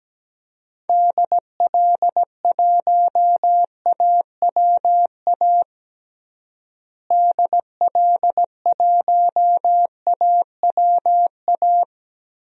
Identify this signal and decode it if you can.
Morse code, heard as a keyed tone, about 17 wpm, 700 Hz. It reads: DL1AWA DL1AWA
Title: DXCC-100 Call Signs - 6 Characters - Repeated 3x - 17wpm